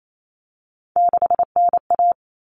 Morse code, heard as a keyed tone, about 28 wpm, 700 Hz. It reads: 6NA
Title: Common Cut Numbers - 3-Digit 28wpm